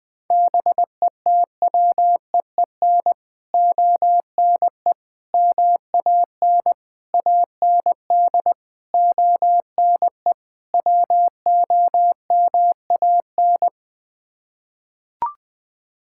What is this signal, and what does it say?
Morse code, heard as a keyed tone, about 20 wpm, 700 Hz. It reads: BETWEEN ONE MAN AND ONE WOMAN
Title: Sets of 6 Words - 20wpm